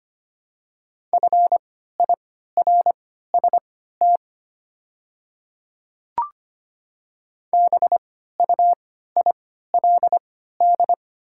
Morse code, heard as a keyed tone, about 25 wpm, spaced as for 14 wpm, 700 Hz. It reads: FIRST BUILD
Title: 3x Character Spacing - Top 200 Words - 25wpm